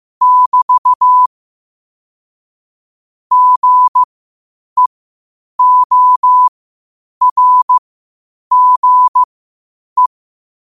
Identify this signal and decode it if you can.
Morse code, heard as a keyed tone, about 15 wpm, spaced as for 8 wpm, 1000 Hz. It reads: <BT> GEORGE
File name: qotdfiles / 8wpm